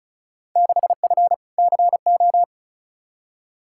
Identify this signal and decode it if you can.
Morse code, heard as a keyed tone, about 36 wpm, 690 Hz. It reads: 6F CO
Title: Field Day Exchanges Encore - 35wpm